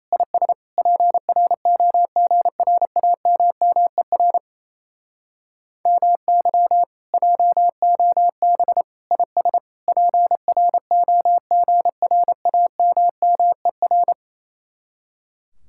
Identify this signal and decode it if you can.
Morse code, heard as a keyed tone, about 30 wpm, 700 Hz. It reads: IS PROGRAMMER MY JOB IS PROGRAMMER
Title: Speed-Racing Common QSO Phrases - 28wpm